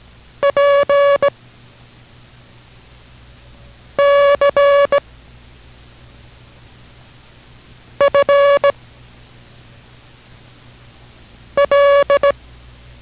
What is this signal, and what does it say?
Morse code, heard as a keyed tone, about 18 wpm, 560 Hz. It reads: P C F L